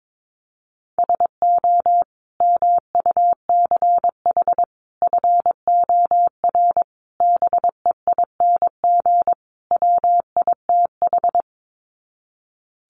Morse code, heard as a keyed tone, about 22 wpm, 700 Hz. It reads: SO MUCH FOR BEING WITH
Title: Sets of 5 Words Encore 22wpm